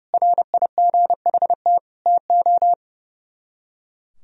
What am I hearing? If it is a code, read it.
Morse code, heard as a keyed tone, about 35 wpm, 700 Hz. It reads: RIGHT TO
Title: Speed-Racing - Sets of 4 Words 30wpm